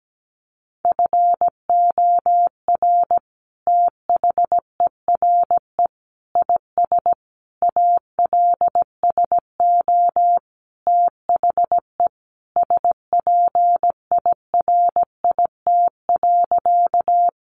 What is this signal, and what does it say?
Morse code, heard as a keyed tone, about 17 wpm, 700 Hz. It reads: FOR THERE IS ALSO THE SPIRIT.
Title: Sentences from Top 1400 Words 17wpm